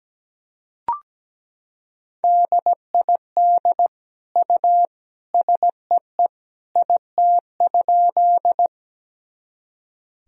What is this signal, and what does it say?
Morse code, heard as a keyed tone, about 17 wpm, 700 Hz. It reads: DID U SEE IT?
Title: Sentences from Top 100 Words and Abbreviations 17wpm